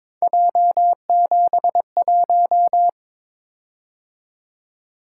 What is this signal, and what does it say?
Morse code, heard as a keyed tone, about 22 wpm, 700 Hz. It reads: J71
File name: practice / 22wpm / Three Letter-Number Combinations - 22wpm